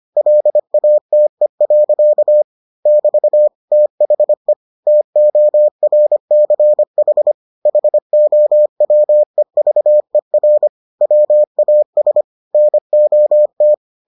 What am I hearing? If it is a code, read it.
Morse code, heard as a keyed tone, about 25 wpm, 600 Hz. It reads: LATE. <BT> THE TORCH HOWEVER WAS NOT